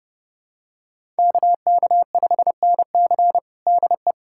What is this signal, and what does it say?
Morse code, heard as a keyed tone, about 30 wpm, 700 Hz. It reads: KK5NC DE
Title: USA Callsign DE Callsign 30wpm